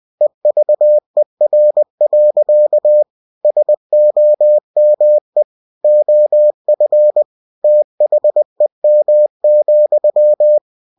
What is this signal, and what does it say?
Morse code, heard as a keyed tone, about 20 wpm, 600 Hz. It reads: EVER. SOME OF THEM,
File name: cw-books / winnie-the-pooh / winnie-20wpm-page-0009